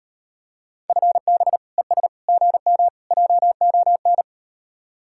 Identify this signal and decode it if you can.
Morse code, heard as a keyed tone, about 38 wpm, 700 Hz. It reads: FB ES GM JON